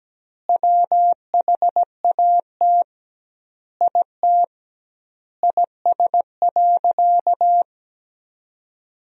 Morse code, heard as a keyed tone, about 17 wpm, 700 Hz. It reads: WHAT IT IS.
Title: 2x Word-Spacing - Sentences from Top 200 Words - 17wpm